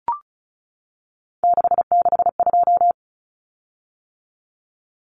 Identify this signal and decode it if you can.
Morse code, heard as a keyed tone, about 34 wpm, 700 Hz. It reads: 662
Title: Common Cut Numbers - 3-Digit 35wpm